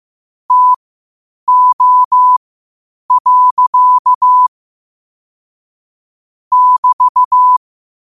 Morse code, heard as a keyed tone, about 15 wpm, spaced as for 8 wpm, 1000 Hz. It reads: TO. <BT>
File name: qotdfiles / 8wpm